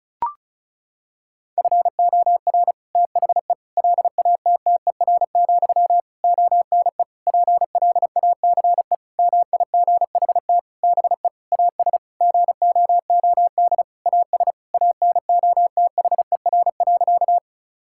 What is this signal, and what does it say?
Morse code, heard as a keyed tone, about 36 wpm, 700 Hz. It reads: FOR THE LATTER, ONE PLACE MIGHT BE AS GOOD AS ANOTHER.
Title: Sentences from Top 1600 Words 35wpm